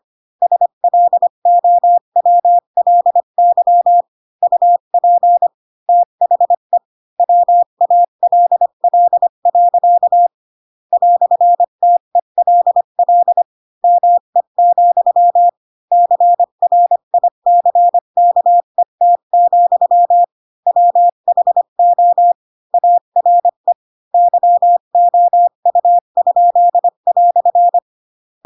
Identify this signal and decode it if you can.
Morse code, heard as a keyed tone, about 25 wpm, 700 Hz. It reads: SLOWLY UP THE WALL. "TELL ME, CRICKET, WHO ARE YOU?"
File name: Pinocchio / Pinocchio ch4 20 25 ik1hge